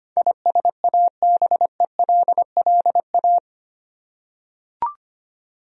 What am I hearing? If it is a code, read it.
Morse code, heard as a keyed tone, about 25 wpm, 700 Hz. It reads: ISABELLA
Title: USA Common Names - 25wpm